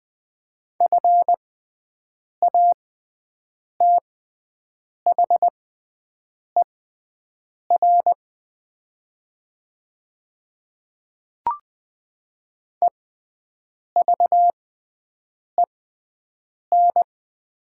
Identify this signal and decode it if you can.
Morse code, heard as a keyed tone, about 20 wpm, spaced as for 7 wpm, 700 Hz. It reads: FATHER EVEN
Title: 6x Character Spacing - Top 200 Words - 20wpm